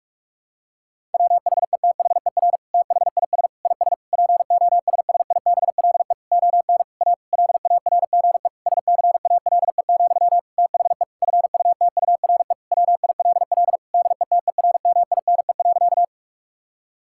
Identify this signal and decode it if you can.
Morse code, heard as a keyed tone, about 46 wpm, 700 Hz. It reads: WHETHER THIS IS POSSIBLE ON A LARGE SCALE, THE FUTURE WILL DETERMINE.
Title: Sentences from Top 1000 Words - Review Part 5 - 45wpm